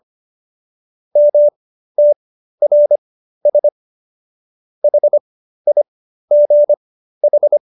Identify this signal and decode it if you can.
Morse code, heard as a keyed tone, about 25 wpm, spaced as for 13 wpm, 590 Hz. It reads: MTRS HIGH